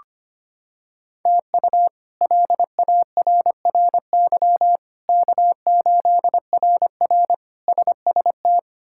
Morse code, heard as a keyed tone, about 25 wpm, 700 Hz. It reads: TU LARRY K8RR SST
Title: K1USN SST QSO Practice - 25wpm